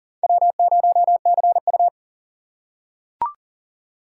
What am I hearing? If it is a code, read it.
Morse code, heard as a keyed tone, about 40 wpm, 700 Hz. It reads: W0CU
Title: USA Call Signs - 4 Characters - 4x Character Spacing - 40wpm